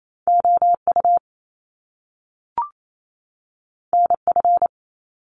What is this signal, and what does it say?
Morse code, heard as a keyed tone, about 28 wpm, 700 Hz. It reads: O U N F
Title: Top Two Letter Combinations in Words 28wpm